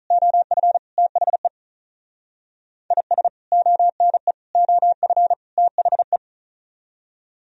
Morse code, heard as a keyed tone, about 46 wpm, 700 Hz. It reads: OF THE IS ONE OF THE